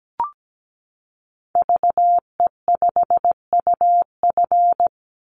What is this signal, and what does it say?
Morse code, heard as a keyed tone, about 17 wpm, 700 Hz. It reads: VE5UF
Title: Call Signs Repeated 2x - 17wpm